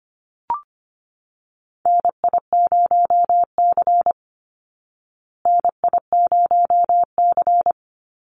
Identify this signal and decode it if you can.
Morse code, heard as a keyed tone, about 25 wpm, 700 Hz. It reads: NI0C NI0C
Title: Call Signs Repeated 3x - 25wpm